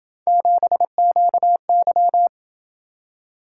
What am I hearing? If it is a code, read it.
Morse code, heard as a keyed tone, about 28 wpm, 700 Hz. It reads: 7QY